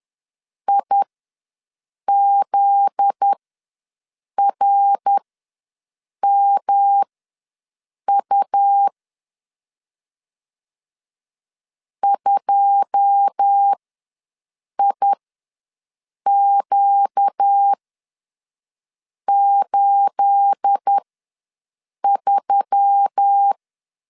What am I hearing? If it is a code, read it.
Morse code, heard as a keyed tone, about 10 wpm, spaced as for 6 wpm, 800 Hz. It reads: IZRMU 2IQ83
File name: LEZIONI / 9 28 ws 10wpm 4124